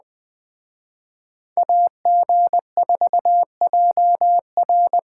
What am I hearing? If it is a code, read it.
Morse code, heard as a keyed tone, about 21 wpm, 700 Hz. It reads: AG4JR
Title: Speed Racing - USA 2x2 Call Signs 17wpm